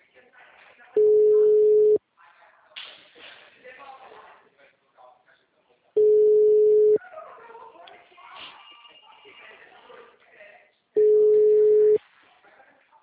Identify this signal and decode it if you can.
Ringback tone (Europe, 425 Hz), with speech.